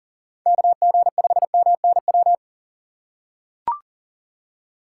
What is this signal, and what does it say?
Morse code, heard as a keyed tone, about 40 wpm, 700 Hz. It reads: KG5MNW
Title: USA Call Signs - 6 Characters - 4x Character Spacing - 40wpm